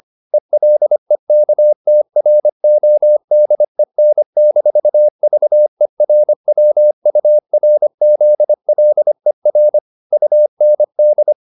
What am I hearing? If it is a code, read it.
Morse code, heard as a keyed tone, about 25 wpm, 600 Hz. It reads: ELEKTRODEN-VERWURZLER UND